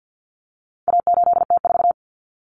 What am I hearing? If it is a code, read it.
Morse code, heard as a keyed tone, about 50 wpm, 700 Hz. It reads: A8T4
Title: Common Cut Numbers - 4-Digit 50wpm